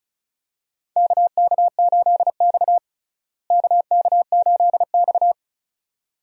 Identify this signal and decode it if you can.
Morse code, heard as a keyed tone, about 36 wpm, 700 Hz. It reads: KK8X KK8X
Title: USA Call Signs - 4 Characters - Repeated 2x - 35wpm